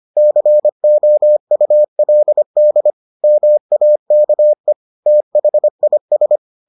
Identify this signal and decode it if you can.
Morse code, heard as a keyed tone, about 25 wpm, 600 Hz. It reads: COULD MAKE THIS